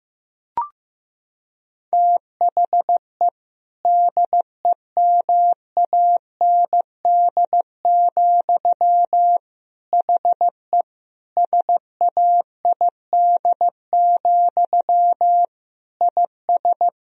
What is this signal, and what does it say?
Morse code, heard as a keyed tone, about 16 wpm, 700 Hz. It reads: THE DEMAND, HE SAID, IS
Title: Sentences from Top 1000 Words - Review Part 4 - 15wpm